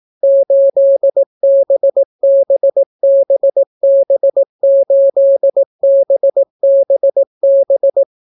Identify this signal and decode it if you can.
Morse code, heard as a keyed tone, about 18 wpm, 550 Hz. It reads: 8BBBB8BBB